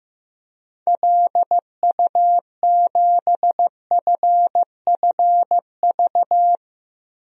Morse code, heard as a keyed tone, about 15 wpm, 700 Hz. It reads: LU7FFV
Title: DXCC-100 Call Signs - 6 Characters - 15wpm